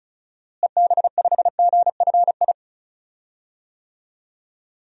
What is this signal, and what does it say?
Morse code, heard as a keyed tone, about 36 wpm, 700 Hz. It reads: EB5GFI
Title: DXCC-100 Call Signs - 6 Characters - Repeated 2x - 35wpm